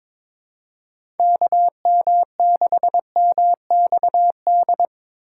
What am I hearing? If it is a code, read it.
Morse code, heard as a keyed tone, about 22 wpm, 700 Hz. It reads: KM6MXD
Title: USA Call Signs - 6 Characters - 3x Character Spacing - 22wpm